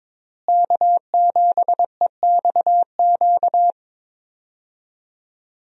Morse code, heard as a keyed tone, about 22 wpm, 700 Hz. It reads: K7EXQ
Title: USA Call Signs - 5 Characters - Repeated 3x - 22wpm